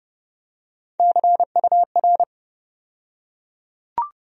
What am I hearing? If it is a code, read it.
Morse code, heard as a keyed tone, about 32 wpm, 700 Hz. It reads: CUR
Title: USA Call Sign Postfixes - 3 Letters 30wpm